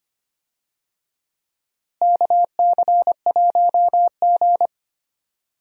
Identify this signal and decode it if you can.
Morse code, heard as a keyed tone, about 25 wpm, 700 Hz. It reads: KC1G
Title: USA Call Signs - 4 Characters - Repeated 3x - 25wpm